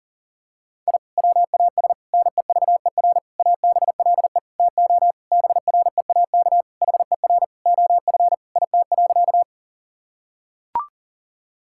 Morse code, heard as a keyed tone, about 40 wpm, 690 Hz. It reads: I WAS NEVER ABLE TO BREAK HER OF IT.
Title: Sentences from Top 1000 Words - Review Part 6 - 40wpm